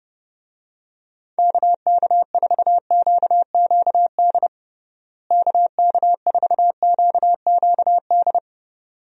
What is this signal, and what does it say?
Morse code, heard as a keyed tone, about 30 wpm, 700 Hz. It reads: KK4QQD KK4QQD
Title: USA Call Signs - 6 Characters - Repeated 2x - 30wpm